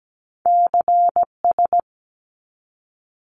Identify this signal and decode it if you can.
Morse code, heard as a keyed tone, about 17 wpm, 700 Hz. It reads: CS
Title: Abbreviations 17wpm